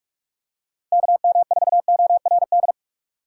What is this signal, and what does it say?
Morse code, heard as a keyed tone, about 46 wpm, 700 Hz. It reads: KM4ORD
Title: USA Call Signs - 6x Character Spacing - 45wpm